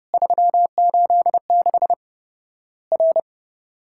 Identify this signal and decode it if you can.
Morse code, heard as a keyed tone, about 30 wpm, 700 Hz. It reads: 386 R
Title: Worked All Europe DX Contest - QTC Exchange - 30wpm